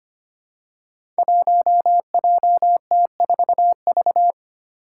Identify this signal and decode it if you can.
Morse code, heard as a keyed tone, about 25 wpm, 700 Hz. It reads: 1JT4V
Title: Five Challenging Letter-Number Combinations 25wpm